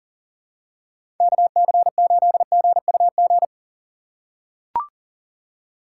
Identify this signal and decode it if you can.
Morse code, heard as a keyed tone, about 40 wpm, 700 Hz. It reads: KC8GUG
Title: USA Call Signs - 3x Character Spacing - 40wpm